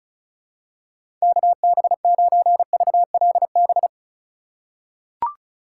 Morse code, heard as a keyed tone, about 34 wpm, 700 Hz. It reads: KB9VLB
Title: USA Call Signs - 6 Characters - 8x Character Spacing - 35wpm